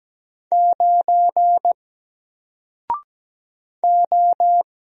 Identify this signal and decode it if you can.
Morse code, heard as a keyed tone, about 17 wpm, 700 Hz. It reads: 9 O
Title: Single Letter-Number Alt 17wpm